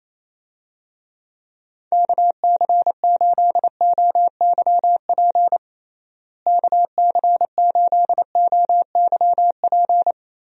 Morse code, heard as a keyed tone, about 28 wpm, 700 Hz. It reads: KC8OYP KC8OYP
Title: USA Call Signs - 6 Characters - Repeated 2x - 28wpm